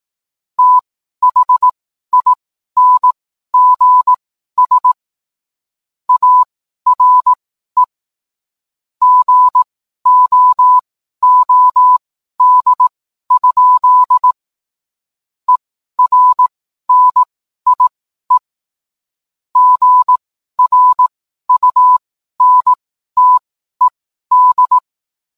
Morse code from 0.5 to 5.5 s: THINGS